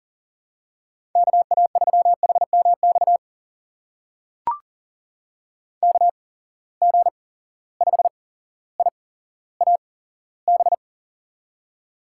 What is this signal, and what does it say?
Morse code, heard as a keyed tone, about 39 wpm, spaced as for 17 wpm, 700 Hz. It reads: KA3HMX KG5IAB
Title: USA Call Signs - 6 Characters - 8x Character Spacing - 40wpm